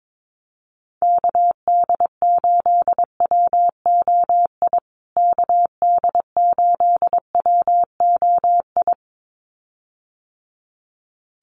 Morse code, heard as a keyed tone, about 22 wpm, 700 Hz. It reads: KD8WOI KD8WOI
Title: USA Call Signs Repeated 2x 22wpm